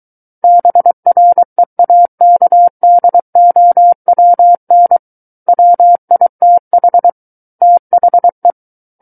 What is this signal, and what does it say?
Morse code, heard as a keyed tone, about 23 wpm, 700 Hz. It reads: BREAKDOWN WITH THE
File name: Book One / 14-Chapter 14 - 23 WPM